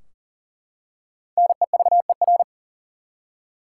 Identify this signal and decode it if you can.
Morse code, heard as a keyed tone, about 40 wpm, 700 Hz. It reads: NEVER
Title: Five Letter Words - 40wpm